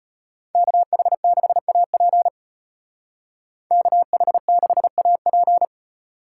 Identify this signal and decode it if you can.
Morse code, heard as a keyed tone, about 31 wpm, 700 Hz. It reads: KH6AP KH6AP
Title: Speed Racing - USA 2x2 Call Signs 25wpm